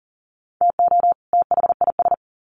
Morse code, heard as a keyed tone, about 40 wpm, 700 Hz. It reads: TO THIS
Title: Sets of 2 Words 40wpm